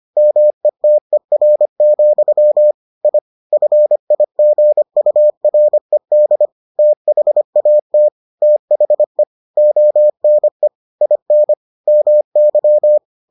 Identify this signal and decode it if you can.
Morse code, heard as a keyed tone, about 25 wpm, 600 Hz. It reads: METER, I FIGURED THAT THE ONE IN MY